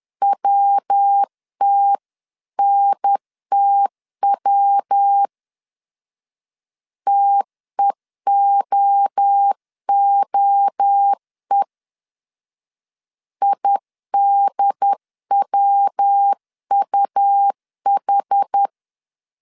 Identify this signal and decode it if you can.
Morse code, heard as a keyed tone, about 10 wpm, 800 Hz. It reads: WTNTW TEOOE IDWUH
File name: 10wpm